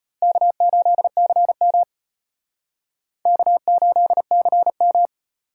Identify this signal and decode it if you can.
Morse code, heard as a keyed tone, about 31 wpm, 700 Hz. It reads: K8CM K8CM
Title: Speed Racing - USA 1x2 Call Signs 25wpm